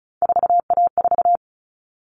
Morse code, heard as a keyed tone, about 34 wpm, 700 Hz. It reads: 4A4
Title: Common Cut Numbers - 3-Digit 35wpm